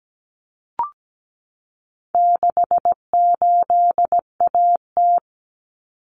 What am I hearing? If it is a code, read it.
Morse code, heard as a keyed tone, about 17 wpm, 700 Hz. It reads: 68AT
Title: Common Cut Numbers - 4-Digit 17wpm